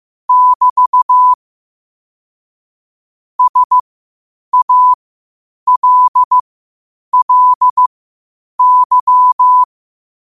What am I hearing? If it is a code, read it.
Morse code, heard as a keyed tone, about 15 wpm, spaced as for 8 wpm, 1000 Hz. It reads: <BT> SALLY
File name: qotdfiles / 8wpm